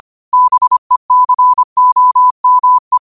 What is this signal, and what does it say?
Morse code, heard as a keyed tone, about 25 wpm, 1000 Hz. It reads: BECOME
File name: qotdfiles / 25wpm